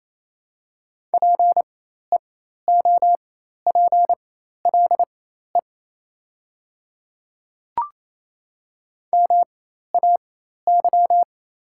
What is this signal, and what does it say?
Morse code, heard as a keyed tone, about 27 wpm, spaced as for 13 wpm, 690 Hz. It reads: PEOPLE MAY
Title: 4x Character Spacing - Top 100 Words - 28wpm